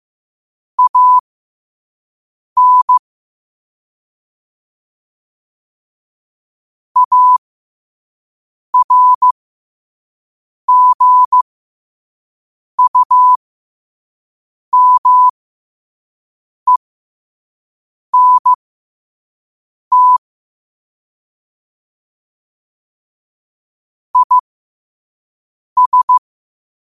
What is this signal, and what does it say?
Morse code, heard as a keyed tone, about 15 wpm, spaced as for 5 wpm, 1000 Hz. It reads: AN ARGUMENT IS
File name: qotdfiles / 5wpm